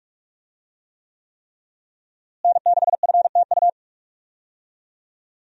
Morse code, heard as a keyed tone, about 42 wpm, 700 Hz. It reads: N6FTU
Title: USA Call Signs - 5 Characters - Repeated 3x - 45wpm